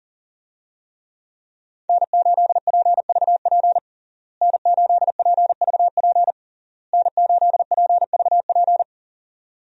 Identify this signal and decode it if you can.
Morse code, heard as a keyed tone, about 40 wpm, 700 Hz. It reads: N8PVP N8PVP N8PVP
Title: USA Call Signs - 5 Characters - Repeated 3x - 40wpm